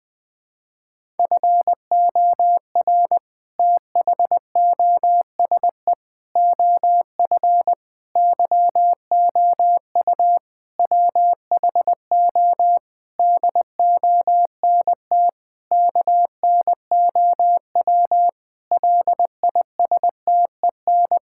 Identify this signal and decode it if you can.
Morse code, heard as a keyed tone, about 20 wpm, 700 Hz. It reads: FOR THOSE OF YOU WHO DONT KNOW LISTEN
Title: Sentences with 7-Word Phrases - Spoken Beginning - 20wpm